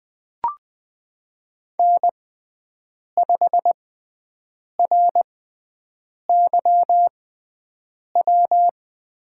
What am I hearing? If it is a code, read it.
Morse code, heard as a keyed tone, about 20 wpm, spaced as for 7 wpm, 700 Hz. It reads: N5RYW
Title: USA Call Signs - 6x Character Spacing - 20wpm